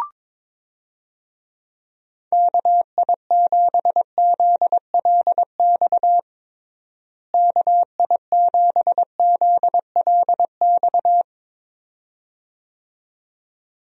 Morse code, heard as a keyed tone, about 22 wpm, 700 Hz. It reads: KI7ZLX KI7ZLX
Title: USA Call Signs - 6 Characters - Repeated 2x - 22wpm